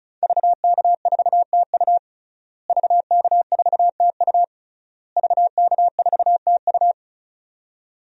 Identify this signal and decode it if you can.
Morse code, heard as a keyed tone, about 34 wpm, 700 Hz. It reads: VK4TU VK4TU VK4TU
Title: DXCC-100 Call Signs - 5 Characters - Repeated 3x - 35wpm